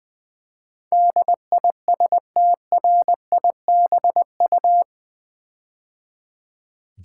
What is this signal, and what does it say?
Morse code, heard as a keyed tone, about 20 wpm, 700 Hz. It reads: DISTRIBU
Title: Build A Word - 4 Letters At A Time - 12-Letter Words - 20wpm